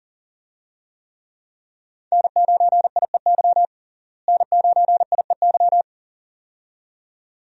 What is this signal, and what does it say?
Morse code, heard as a keyed tone, about 37 wpm, 700 Hz. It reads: N9IEY N9IEY
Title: USA Call Signs - 5 Characters - Repeated 2x - 40wpm